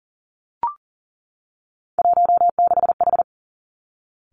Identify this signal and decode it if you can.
Morse code, heard as a keyed tone, about 38 wpm, 700 Hz. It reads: 16H
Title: Three Challenging Letter-Number Combinations 40wpm